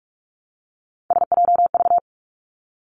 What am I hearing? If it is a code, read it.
Morse code, heard as a keyed tone, about 46 wpm, 700 Hz. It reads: SJV
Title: ARRL Sections 45wpm